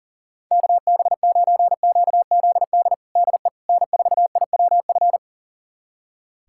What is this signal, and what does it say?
Morse code, heard as a keyed tone, about 40 wpm, 700 Hz. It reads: KB9QZD DE N4IWF
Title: USA Callsign DE Callsign 40wpm